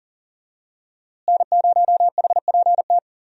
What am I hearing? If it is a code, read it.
Morse code, heard as a keyed tone, about 40 wpm, 700 Hz. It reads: N0HPT
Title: USA Call Signs - 5 Characters - Repeated 3x - 40wpm